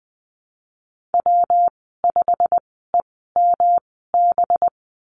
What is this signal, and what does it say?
Morse code, heard as a keyed tone, about 20 wpm, spaced as for 15 wpm, 700 Hz. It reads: W5EMB
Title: 219 Lesson 35g - Number 0 - Call Signs 20wpm